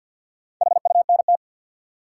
Morse code, heard as a keyed tone, about 52 wpm, 700 Hz. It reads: HUNT
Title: Four Letter Words - 50wpm